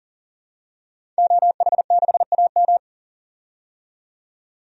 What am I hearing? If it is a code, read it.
Morse code, heard as a keyed tone, about 40 wpm, 700 Hz. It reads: OH6AM
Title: DXCC-100 Call Signs - 5 Characters - 40wpm